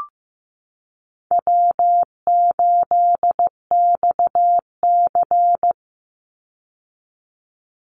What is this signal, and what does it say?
Morse code, heard as a keyed tone, about 15 wpm, 700 Hz. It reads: W8XC
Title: Easy Call Signs - 15wpm